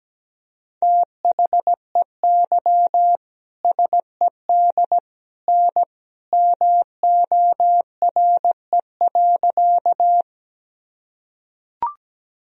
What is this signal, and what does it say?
Morse code, heard as a keyed tone, about 17 wpm, 700 Hz. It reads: THEY SED N MORE.
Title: Sentences from Top 100 Words and Abbreviations 17wpm